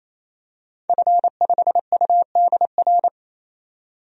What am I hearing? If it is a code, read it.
Morse code, heard as a keyed tone, about 27 wpm, 700 Hz. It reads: F5UDR
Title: DXCC-100 Call Signs - 5 Characters - 28wpm